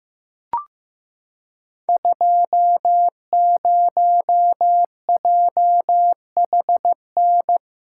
Morse code, heard as a keyed tone, about 15 wpm, 700 Hz. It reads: 20JHN
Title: Code Groups - 5 Letter:Numbers - 15wpm